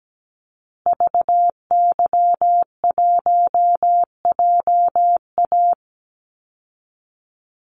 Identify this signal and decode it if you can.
Morse code, heard as a keyed tone, about 17 wpm, 700 Hz. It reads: VY1JA
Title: Call Signs Repeated 3x - 17wpm